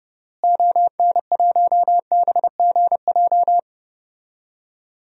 Morse code, heard as a keyed tone, about 30 wpm, 700 Hz. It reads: ON1BGJ
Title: DXCC-100 Call Signs - 6 Characters - Repeated 2x - 30wpm